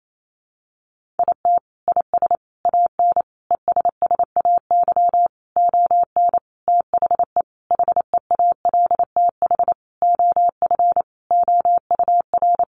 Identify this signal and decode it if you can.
Morse code, heard as a keyed tone, about 28 wpm, 700 Hz. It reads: IT IS AN ESSAY ON THE HEALTH OF OUR
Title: Sentences from Top 1500 Words 28wpm